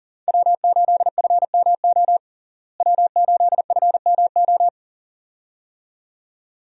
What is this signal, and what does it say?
Morse code, heard as a keyed tone, about 40 wpm, 700 Hz. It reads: W8FMO W8FMO
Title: USA Call Signs - 5 Characters - Repeated 2x - 40wpm